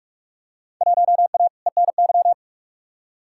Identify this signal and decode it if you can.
Morse code, heard as a keyed tone, about 45 wpm, 700 Hz. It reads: 1A ENY
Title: Field Day Exchanges Encore - 45wpm